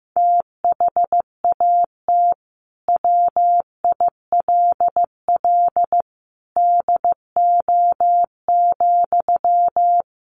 Morse code, heard as a keyed tone, about 15 wpm, 700 Hz. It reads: THAT WILL DO,
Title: Sentences from Top 100 Words 15wpm